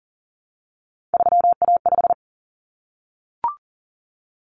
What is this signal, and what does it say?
Morse code, heard as a keyed tone, about 40 wpm, 700 Hz. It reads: 3A5
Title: Common Cut Numbers - 3-Digit 40wpm